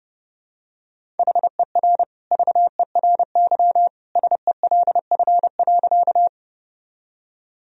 Morse code, heard as a keyed tone, about 30 wpm, 700 Hz. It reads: HER VERY SELF.
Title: Sentences from Top 1000 Words - Review Part 1 - 30wpm